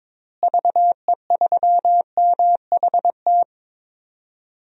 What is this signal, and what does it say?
Morse code, heard as a keyed tone, about 22 wpm, 700 Hz. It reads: VE3MHT
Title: DXCC-100 Call Signs - 6 Characters - Repeated 3x - 22wpm